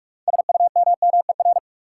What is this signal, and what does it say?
Morse code, heard as a keyed tone, about 46 wpm, 700 Hz. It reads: SUMMER
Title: Sentences from Top 1000 Words - Review Part 6 - New Word Drill - 45wpm